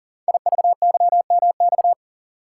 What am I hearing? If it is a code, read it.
Morse code, heard as a keyed tone, about 40 wpm, 690 Hz. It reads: IVYMX
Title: Code Groups - 5 Letters - 40wpm